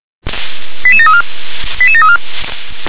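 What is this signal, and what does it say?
Five-tone selective call (ZVEI), heard as quick runs of five tones, tones of about 70 ms. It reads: 88734 88734
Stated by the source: ZVEI five-tone selective call received by SDR.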